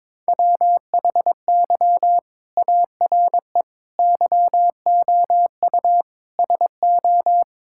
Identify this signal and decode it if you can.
Morse code, heard as a keyed tone, about 22 wpm, 700 Hz. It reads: WHY ARE YOU SO
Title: Sentences from Top 1000 Words - Review Part 6 - 22wpm